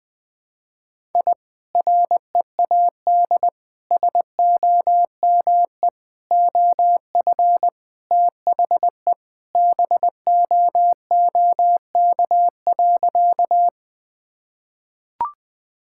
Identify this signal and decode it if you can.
Morse code, heard as a keyed tone, about 20 wpm, 700 Hz. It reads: I READ SOME OF THE BOOK.
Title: Sentences with 2-Word Phrases - 20wpm